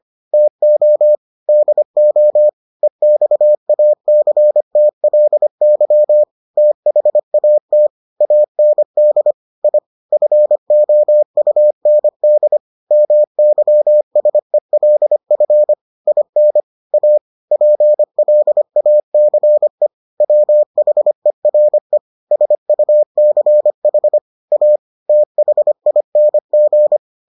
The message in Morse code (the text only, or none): TO DO EXACTLY THAT AND I FOUND MYSELF IN A PLACE WHERE SUCH A THING